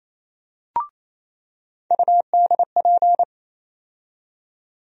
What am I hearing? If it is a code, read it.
Morse code, heard as a keyed tone, about 28 wpm, 700 Hz. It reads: UDP
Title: USA Call Sign Postfixes - 3 Letters 28wpm